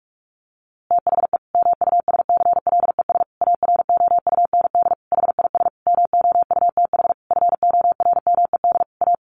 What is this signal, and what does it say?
Morse code, heard as a keyed tone, about 44 wpm, 700 Hz. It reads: THE MUSCLES AROUND HIS MOUTH FORMED A
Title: A Sound Of Thunder 45wpm